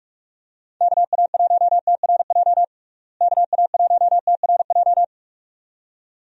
Morse code, heard as a keyed tone, about 46 wpm, 700 Hz. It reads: KA1TRJ KA1TRJ
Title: USA Call Signs - 6 Characters - Repeated 2x - 45wpm